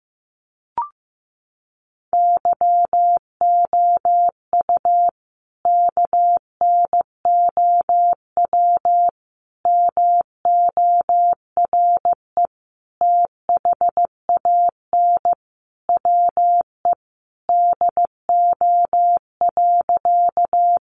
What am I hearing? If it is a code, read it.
Morse code, heard as a keyed tone, about 15 wpm, 700 Hz. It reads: YOU KNOW MORE THAN WE DO.
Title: Sentences from Top 100 Words 15wpm